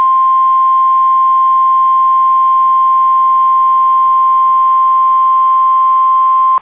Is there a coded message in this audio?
Yes: Emergency Alert System (SAME) tones (the 1050 Hz attention tone) from 0.2 to 5.9 s.